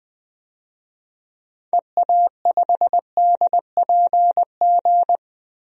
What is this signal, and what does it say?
Morse code, heard as a keyed tone, about 21 wpm, 700 Hz. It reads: EA5DPG
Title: DXCC-100 Call Signs - 6 Characters - 20wpm